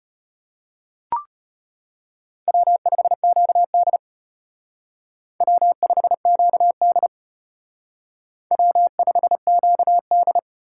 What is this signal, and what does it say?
Morse code, heard as a keyed tone, about 33 wpm, 700 Hz. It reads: W5QD W5QD W5QD